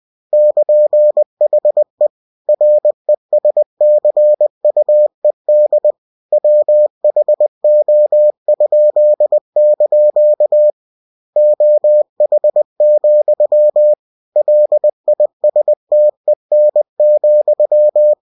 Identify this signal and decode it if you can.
Morse code, heard as a keyed tone, about 20 wpm, 600 Hz. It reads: <KN>HE RESCUED WHO?) OH, LISTEN,
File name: cw-books / winnie-the-pooh / winnie-20wpm-page-0103